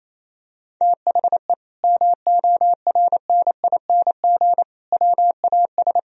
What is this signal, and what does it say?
Morse code, heard as a keyed tone, about 28 wpm, 700 Hz. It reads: THE MORNING WAS
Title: Sentences from Top 1000 Words - Review Part 3 - 28wpm